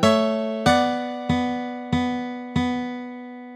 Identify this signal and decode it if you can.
There is no code.